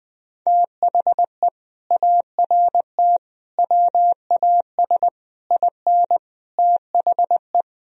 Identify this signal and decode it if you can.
Morse code, heard as a keyed tone, about 20 wpm, 700 Hz. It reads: THE ART WAS IN THE
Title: Sentences from Top 1000 Words - Review Part 3 - 20wpm